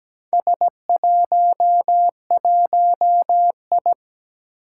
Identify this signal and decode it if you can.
Morse code, heard as a keyed tone, about 17 wpm, 700 Hz. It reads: S11I
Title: Four Challenging Letter-Number Combinations 17wpm